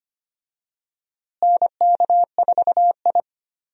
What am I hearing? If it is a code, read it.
Morse code, heard as a keyed tone, about 25 wpm, 700 Hz. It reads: NK4I
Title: USA Call Signs - 4 Characters - Repeated 2x - 25wpm